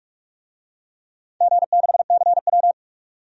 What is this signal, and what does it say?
Morse code, heard as a keyed tone, about 47 wpm, 700 Hz. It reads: G6CW
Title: DXCC-100 Call Signs - 4 Characters - 45wpm